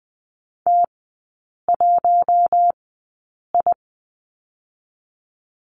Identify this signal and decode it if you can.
Morse code, heard as a keyed tone, about 20 wpm, spaced as for 8 wpm, 700 Hz. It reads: T1I
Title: 041 Lesson 8e - Number 1 - Three Characters 20wpm